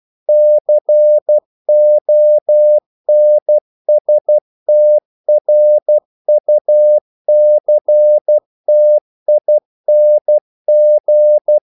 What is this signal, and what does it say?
Morse code, heard as a keyed tone, about 12 wpm, 600 Hz. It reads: CONSTRUCTING
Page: English quotations in CW (telegraphy - morse)